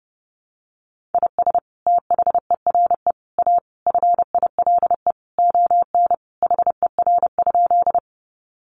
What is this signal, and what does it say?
Morse code, heard as a keyed tone, about 30 wpm, 700 Hz. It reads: IS THERE A FILE ON HER?
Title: Sentences from Top 1200 Words 30wpm